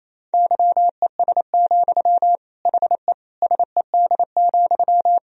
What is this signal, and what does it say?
Morse code, heard as a keyed tone, about 28 wpm, 700 Hz. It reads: YES, HE SED,
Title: Sentences from Top 500 Words and Abbreviations 28wpm